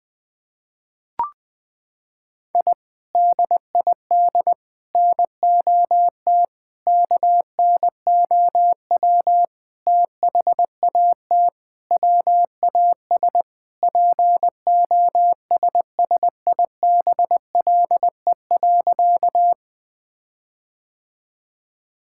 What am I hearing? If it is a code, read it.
Morse code, heard as a keyed tone, about 21 wpm, 700 Hz. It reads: I DID NOT KNOW THAT WAS POSSIBLE.
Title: Sentences with 5-Word Phrases - 20wpm